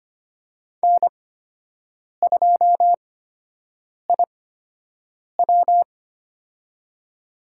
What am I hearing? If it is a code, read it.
Morse code, heard as a keyed tone, about 25 wpm, spaced as for 7 wpm, 700 Hz. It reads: N2IW
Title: USA Call Signs - 4 Characters - 8x Character Spacing - 25wpm